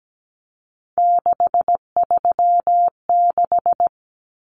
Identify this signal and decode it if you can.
Morse code, heard as a keyed tone, about 17 wpm, 700 Hz. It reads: 636
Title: Common Cut Numbers - 3-Digit 17wpm